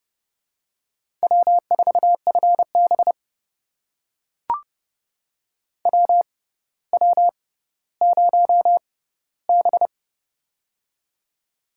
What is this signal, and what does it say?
Morse code, heard as a keyed tone, about 30 wpm, spaced as for 15 wpm, 690 Hz. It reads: W4FB WW0B
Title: USA Call Signs - 4 Characters - 6x Character Spacing - 30wpm